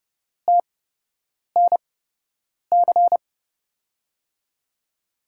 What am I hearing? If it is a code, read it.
Morse code, heard as a keyed tone, about 31 wpm, spaced as for 8 wpm, 700 Hz. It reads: TNC
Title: USA Call Signs - 6 Characters - 8x Character Spacing - 30wpm